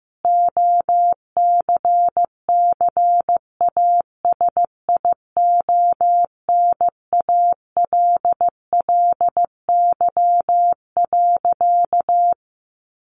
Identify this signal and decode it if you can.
Morse code, heard as a keyed tone, about 15 wpm, 700 Hz. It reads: OCCASIONALLY.
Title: Sentences from Top 1300 Words 15wpm